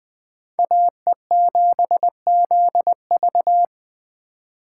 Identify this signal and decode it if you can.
Morse code, heard as a keyed tone, about 20 wpm, 700 Hz. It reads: AE7ZV
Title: Speed Racing - USA 2x2 Call Signs 17wpm